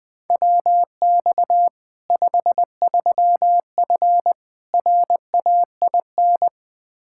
Morse code, heard as a keyed tone, about 21 wpm, 700 Hz. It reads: WX 53F RAIN
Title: Practice QSO - Exchange 2 - Info - 20wpm